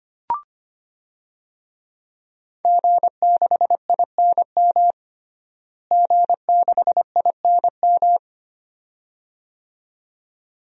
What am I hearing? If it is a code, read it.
Morse code, heard as a keyed tone, about 25 wpm, 700 Hz. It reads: G6INM G6INM
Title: DXCC-100 Call Signs - 5 Characters - Repeated 2x - 25wpm